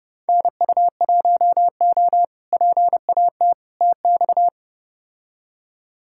Morse code, heard as a keyed tone, about 32 wpm, 700 Hz. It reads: NU1O PAT TX
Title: K1USN SST QSO Practice - 30wpm